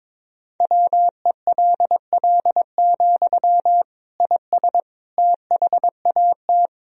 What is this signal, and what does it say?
Morse code, heard as a keyed tone, about 22 wpm, 700 Hz. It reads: WELL, IS THAT
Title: Sentences from Top 1000 Words - Review Part 1 - 22wpm